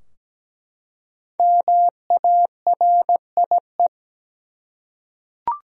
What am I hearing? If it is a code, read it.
Morse code, heard as a keyed tone, about 17 wpm, 700 Hz. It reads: MARIE
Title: USA Common Names - 17wpm